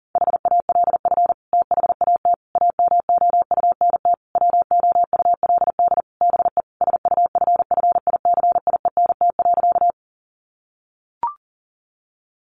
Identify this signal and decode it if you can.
Morse code, heard as a keyed tone, about 40 wpm, 700 Hz. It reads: HALF THAT AMOUNT WOULD BE SUFFICIENT.
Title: Sentences from Top 1400 Words 40wpm